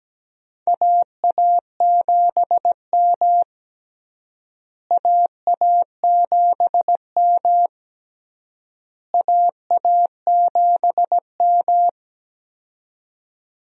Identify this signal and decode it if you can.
Morse code, heard as a keyed tone, about 17 wpm, 700 Hz. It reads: AA7M AA7M AA7M
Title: USA Call Signs - 4 Characters - Repeated 3x - 17wpm